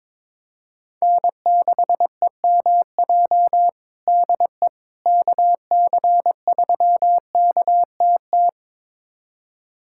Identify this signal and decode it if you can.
Morse code, heard as a keyed tone, about 22 wpm, 700 Hz. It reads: N6EMJ DE KC3KTT
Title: USA Callsign DE Callsign 22wpm